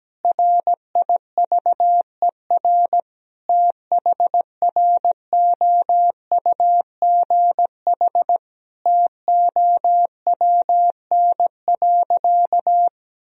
Morse code, heard as a keyed tone, about 17 wpm, 700 Hz. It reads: RIVER THROUGH TOWN.
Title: Sentences from Top 1000 Words - Review Part 2 - 17wpm